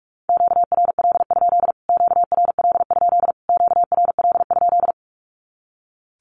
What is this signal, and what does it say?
Morse code, heard as a keyed tone, about 46 wpm, 700 Hz. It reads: QRL? QRL? QRL?
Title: Q-Codes Repeated 3x 45wpm